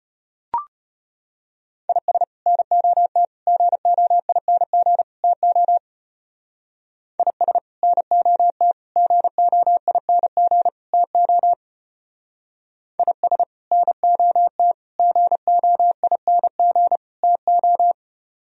Morse code, heard as a keyed tone, about 30 wpm, 700 Hz. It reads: IS NOT GOING TO IS NOT GOING TO IS NOT GOING TO
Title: Speed-Racing - Sets of 4 Words 25wpm